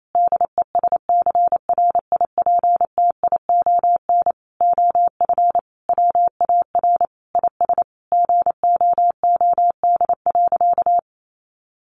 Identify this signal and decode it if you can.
Morse code, heard as a keyed tone, about 28 wpm, 700 Hz. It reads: DESCRIPTION OF WAR IS GOOD.
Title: Sentences from Top 1300 Words 28wpm